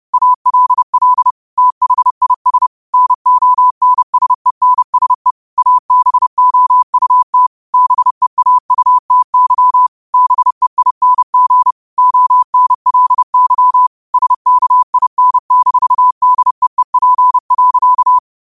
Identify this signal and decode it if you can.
Morse code, heard as a keyed tone, about 30 wpm, 1000 Hz. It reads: ALL THIS NONSENSE ABOUT BEAUTY BEING ONLY SKIN-DEEP.